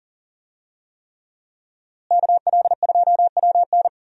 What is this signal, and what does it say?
Morse code, heard as a keyed tone, about 40 wpm, 700 Hz. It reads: KL2WN
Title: USA Call Signs - 5 Characters - Repeated 3x - 40wpm